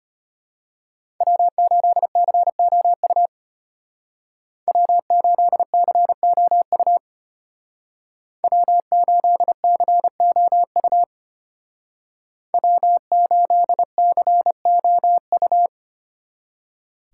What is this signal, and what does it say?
Morse code, heard as a keyed tone, about 33 wpm, 700 Hz. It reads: W8COU W8COU W8COU W8COU